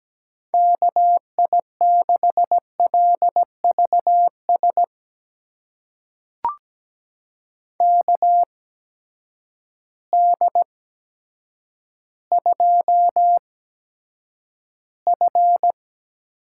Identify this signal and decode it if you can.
Morse code, heard as a keyed tone, about 17 wpm, spaced as for 8 wpm, 700 Hz. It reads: KI6LVS KD2F
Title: USA Call Signs - 6 Characters - 8x Character Spacing - 17wpm